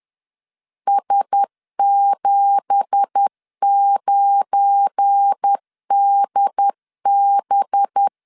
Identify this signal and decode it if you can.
Morse code, heard as a keyed tone, about 10 wpm, 800 Hz. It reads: S79DB
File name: LEZIONI / 9 2867 10wpm 2733